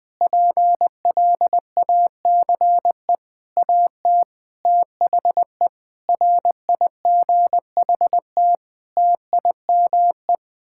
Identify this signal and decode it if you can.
Morse code, heard as a keyed tone, about 20 wpm, 700 Hz. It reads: PLACE AT THE RIGHT TIME
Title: Sets of 8 Words - 20wpm